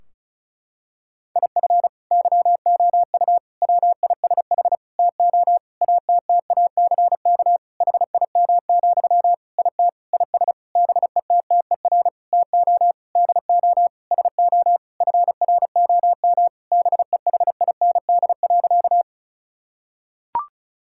Morse code, heard as a keyed tone, about 34 wpm, 700 Hz. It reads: IF YOU WISH TO ATTACK HIM, IT IS BETTER TO DO SO FROM BEHIND.
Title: Sentences from Top 1600 Words 35wpm